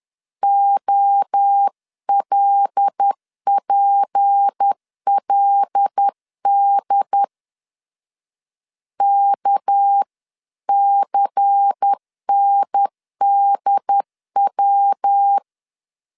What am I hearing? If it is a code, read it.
Morse code, heard as a keyed tone, about 10 wpm, 800 Hz. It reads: OLPLD KCNDW
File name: LEZIONI / 6 PKX0 / 6 PK 10wpm 2436